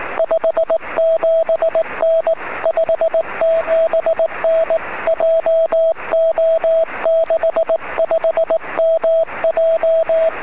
Morse code, heard as a keyed tone, about 18 wpm, 650 Hz, with radio noise. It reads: 57N 57N JO65MJ
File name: sm7lcb / 20-sk7mw 23cm